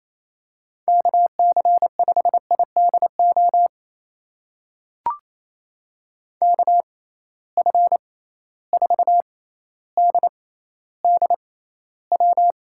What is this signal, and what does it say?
Morse code, heard as a keyed tone, about 28 wpm, spaced as for 14 wpm, 700 Hz. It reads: KC5IDO KF4DDW
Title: USA Call Signs - 6 Characters - 6x Character Spacing - 28wpm